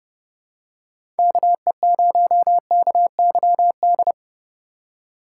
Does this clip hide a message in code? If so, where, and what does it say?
Yes: Morse code from 1.2 to 4.1 s, heard as a keyed tone, about 28 wpm, 700 Hz. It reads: KE0KYD